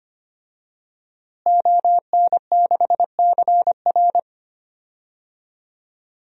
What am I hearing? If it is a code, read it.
Morse code, heard as a keyed tone, about 25 wpm, 700 Hz. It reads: ON6CR
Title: DXCC-100 Call Signs - 5 Characters - Repeated 2x - 25wpm